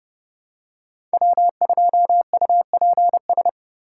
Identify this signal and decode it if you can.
Morse code, heard as a keyed tone, about 30 wpm, 700 Hz. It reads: W2UPS
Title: USA Call Signs - 5 Characters - 3x Character Spacing - 30wpm